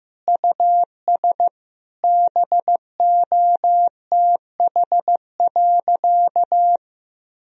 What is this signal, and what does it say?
Morse code, heard as a keyed tone, about 15 wpm, 700 Hz. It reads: US BOTH.
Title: Sentences from Top 1000 Words - Review Part 4 - 15wpm